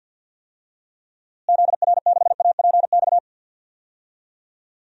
Morse code, heard as a keyed tone, about 53 wpm, 700 Hz. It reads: ZR6APX
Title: DXCC-100 Call Signs - 6 Characters - Repeated 2x - 50wpm